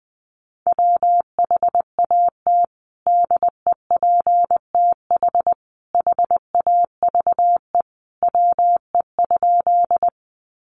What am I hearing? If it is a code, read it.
Morse code, heard as a keyed tone, about 19 wpm, 700 Hz. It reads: WHAT DEPTH HAVE WE?
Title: Sentences from Top 1200 Words 20wpm